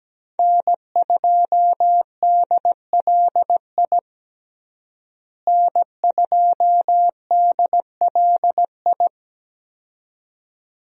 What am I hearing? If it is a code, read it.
Morse code, heard as a keyed tone, about 17 wpm, 700 Hz. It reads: N2DLI N2DLI
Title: USA Call Signs - 5 Characters - Repeated 2x - 17wpm